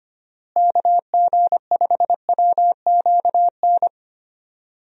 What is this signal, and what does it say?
Morse code, heard as a keyed tone, about 25 wpm, 700 Hz. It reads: KG5WQN
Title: USA Call Signs - 6 Characters - Repeated 3x - 25wpm